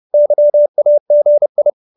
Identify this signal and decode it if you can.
Morse code, heard as a keyed tone, about 30 wpm, 590 Hz. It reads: YAGI